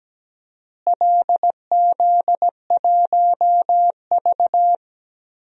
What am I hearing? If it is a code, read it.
Morse code, heard as a keyed tone, about 17 wpm, 700 Hz. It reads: LZ1V
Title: DXCC-100 Call Signs - 4 Characters - Repeated 2x - 17wpm